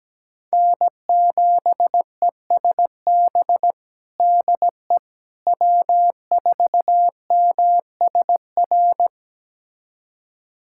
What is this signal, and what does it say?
Morse code, heard as a keyed tone, about 17 wpm, 700 Hz. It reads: N7ESB DE W4MSR
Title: USA Callsign DE Callsign - 5 Characters - 17wpm